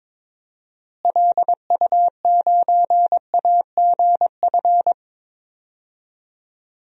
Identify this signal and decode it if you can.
Morse code, heard as a keyed tone, about 21 wpm, 700 Hz. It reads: LU9AGF
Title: DXCC-100 Call Signs - 6 Characters - Repeated 2x - 22wpm